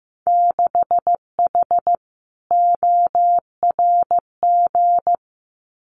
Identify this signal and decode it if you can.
Morse code, heard as a keyed tone, about 15 wpm, 700 Hz. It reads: 6H ORG
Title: Winter Field Day Exchanges Encore 15wpm